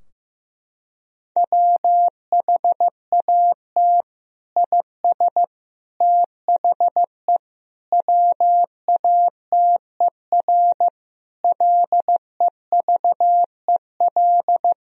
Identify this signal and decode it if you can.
Morse code, heard as a keyed tone, about 14 wpm, 700 Hz. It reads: WHAT IS THE WATER LEVEL
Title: Sentences from Top 1000 Words - Review Part 4 - 15wpm